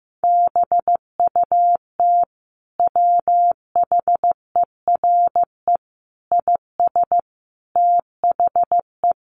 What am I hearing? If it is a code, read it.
Morse code, heard as a keyed tone, about 15 wpm, 700 Hz. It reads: BUT WHERE IS THE
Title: Sentences from Top 1500 Words 15wpm